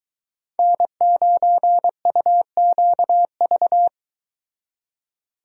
Morse code, heard as a keyed tone, about 22 wpm, 700 Hz. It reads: N9UQV